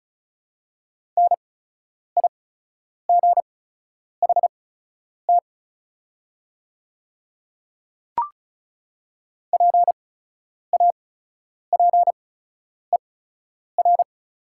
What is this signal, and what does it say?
Morse code, heard as a keyed tone, about 34 wpm, spaced as for 10 wpm, 700 Hz. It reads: NIGHT PAPER
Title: 8x Character Spacing - Top 300 Words - 35wpm